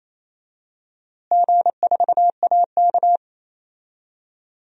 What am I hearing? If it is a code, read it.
Morse code, heard as a keyed tone, about 28 wpm, 700 Hz. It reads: G4AK
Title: DXCC-100 Call Signs - 4 Characters - Repeated 2x - 28wpm